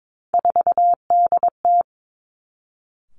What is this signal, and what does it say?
Morse code, heard as a keyed tone, about 22 wpm, 700 Hz. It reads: 4DT
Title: Three Challenging Letter-Number Combinations 22wpm